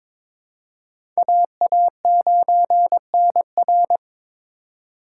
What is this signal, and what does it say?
Morse code, heard as a keyed tone, about 22 wpm, 700 Hz. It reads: AA9NR
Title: USA Call Signs - 4x Character Spacing - 22wpm